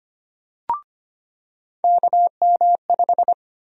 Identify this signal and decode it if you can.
Morse code, heard as a keyed tone, about 25 wpm, 700 Hz. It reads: KM5
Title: USA Call Sign Prefixes - 25wpm